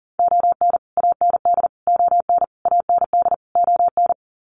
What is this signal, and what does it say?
Morse code, heard as a keyed tone, about 40 wpm, 700 Hz. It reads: ON AND ON AND ON
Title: Sets of 5 Words Encore2 40wpm